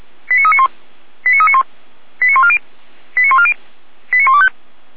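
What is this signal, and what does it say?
Five-tone selective call (ZVEI), heard as quick runs of five tones, tones of about 70 ms. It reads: 89391 89391 89140 89140 89125
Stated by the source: Five-tone ZVEI selective call 89058, received via SDR by feuerwehr-orlamuende fire brigade.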